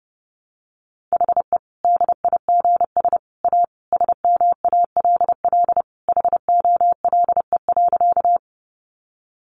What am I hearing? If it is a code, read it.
Morse code, heard as a keyed tone, about 30 wpm, 700 Hz. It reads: HE DIGS A SMALL HOLE.
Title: Easy Phrases 30wpm